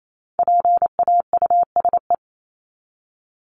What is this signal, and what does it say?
Morse code, heard as a keyed tone, about 28 wpm, 700 Hz. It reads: PAUSE
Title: Top 1501-1750 Words 28wpm